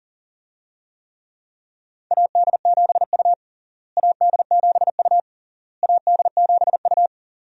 Morse code, heard as a keyed tone, about 40 wpm, 700 Hz. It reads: AD7U AD7U AD7U
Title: USA Call Signs - 4 Characters - Repeated 3x - 40wpm